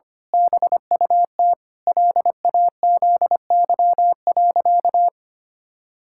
Morse code, heard as a keyed tone, about 25 wpm, 700 Hz. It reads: BUT LAZY.
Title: Pangram Challenge v2 - 25wpm